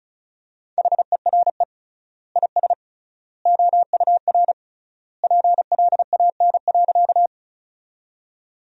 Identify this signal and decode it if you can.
Morse code, heard as a keyed tone, about 36 wpm, 700 Hz. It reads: HERE IS OUR PLAN.
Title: 3x Word-Spacing - Sentences from Top 500 Words - 35wpm